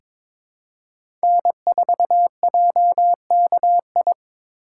Morse code, heard as a keyed tone, about 23 wpm, 700 Hz. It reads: N4JKI
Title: USA Call Signs - 5 Characters - 3x Character Spacing - 22wpm